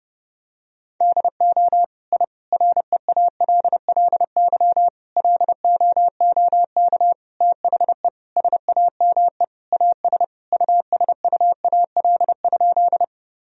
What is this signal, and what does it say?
Morse code, heard as a keyed tone, about 30 wpm, 700 Hz. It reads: DO I REALLY LOOK THE SAME AS USUAL?
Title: Sentences from Top 1000 Words - Review Part 2 - 30wpm